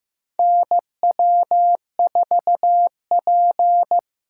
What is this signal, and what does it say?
Morse code, heard as a keyed tone, about 15 wpm, 700 Hz. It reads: NW4P
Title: USA Call Signs - 4 Characters - 8x Character Spacing - 15wpm